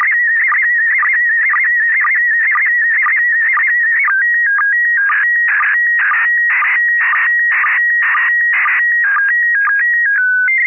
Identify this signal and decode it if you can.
SSTV (slow-scan television), PD 120.
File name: QSL Museo Prefectura PD120